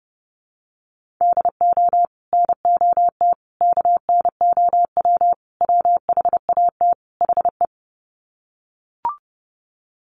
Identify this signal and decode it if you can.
Morse code, heard as a keyed tone, about 30 wpm, 700 Hz. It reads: DO NOT KNOW WHAT HE
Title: Sets of 5 Words Encore2 30wpm